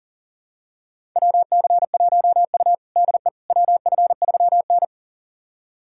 Morse code, heard as a keyed tone, about 40 wpm, 700 Hz. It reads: WC1U DE WF3N
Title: USA Callsign DE Callsign - 4 Characters - 40wpm